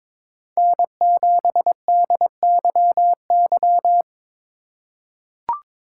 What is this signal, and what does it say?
Morse code, heard as a keyed tone, about 22 wpm, 700 Hz. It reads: N7DYY
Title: USA Call Signs - 5 Characters - 3x Character Spacing - 22wpm